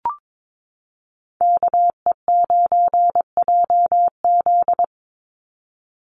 Morse code, heard as a keyed tone, about 22 wpm, 700 Hz. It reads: KE9JZ
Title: USA Call Signs 22wpm